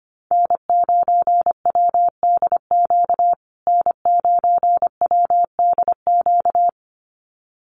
Morse code, heard as a keyed tone, about 25 wpm, 700 Hz. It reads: N9WDQ N9WDQ
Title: USA Call Signs Repeated 2x 25wpm